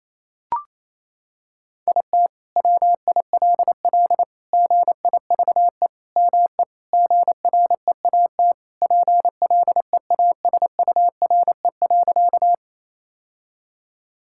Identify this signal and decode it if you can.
Morse code, heard as a keyed tone, about 28 wpm, 700 Hz. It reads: IT WILL GIVE ME GREAT PLEASURE.
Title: Sentences from Top 1600 Words 28wpm